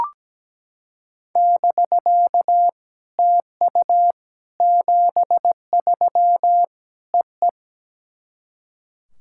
Morse code, heard as a keyed tone, about 17 wpm, 700 Hz. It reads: <BK> TU 73 E E
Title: SOTA QSO Practice 17wpm